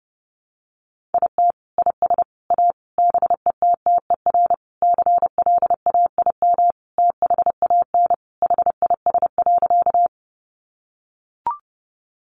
Morse code, heard as a keyed tone, about 30 wpm, 700 Hz. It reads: IT IS A BETTER CLAIM THAN HIS.
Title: Sentences from Top 1500 Words 30wpm